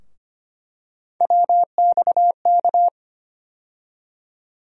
Morse code, heard as a keyed tone, about 25 wpm, 700 Hz. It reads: WXK
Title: USA Call Sign Postfixes - 3 Letters 25wpm